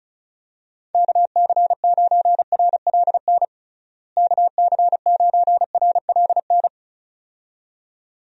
Morse code, heard as a keyed tone, about 36 wpm, 700 Hz. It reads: KC9RLN KC9RLN
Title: USA Call Signs - 6 Characters - Repeated 2x - 35wpm